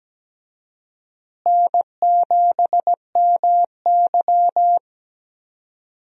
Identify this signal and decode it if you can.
Morse code, heard as a keyed tone, about 17 wpm, 700 Hz. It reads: N7MY
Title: USA Call Signs - 4 Characters - Repeated 2x - 17wpm